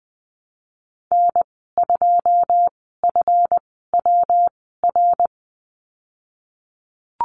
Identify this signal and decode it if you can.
Morse code, heard as a keyed tone, about 20 wpm, spaced as for 15 wpm, 700 Hz. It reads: N2FWR
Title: 186 Lesson 30g - Letter b - Call Signs 20wpm